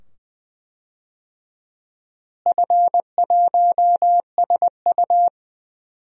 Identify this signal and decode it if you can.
Morse code, heard as a keyed tone, about 20 wpm, 700 Hz. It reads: F1SU
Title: DXCC-100 Call Signs - 4 Characters - Repeated 2x - 20wpm